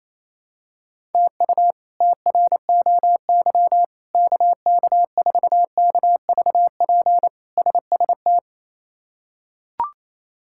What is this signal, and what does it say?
Morse code, heard as a keyed tone, about 28 wpm, 700 Hz. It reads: TU TROY KK4KVP SST
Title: K1USN SST QSO Practice - 28wpm